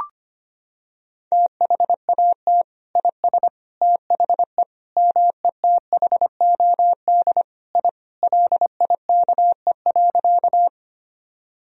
Morse code, heard as a keyed tone, about 25 wpm, 700 Hz. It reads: THAT IS THE METHOD I LIKE.
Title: Sentences from Top 1000 Words - Review Part 3 - 25wpm